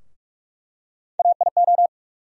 Morse code, heard as a keyed tone, about 42 wpm, 700 Hz. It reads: AIO
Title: USA Call Sign Postfixes - 3 Letters 45wpm